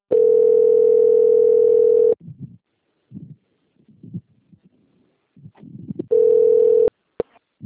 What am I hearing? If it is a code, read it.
Ringback tone (North America), 440 + 480 Hz.
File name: out- - - - - .31847